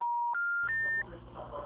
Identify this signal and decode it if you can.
Telephone call progress tone, heard as a telephone network's tone, with noise. Special information tone.